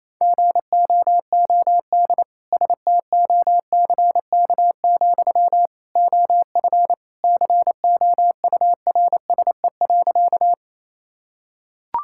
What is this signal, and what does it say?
Morse code, heard as a keyed tone, about 28 wpm, 700 Hz. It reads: GOOD STOCK, OF COURSE.
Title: Sentences from Top 1000 Words - Review Part 5 - 28wpm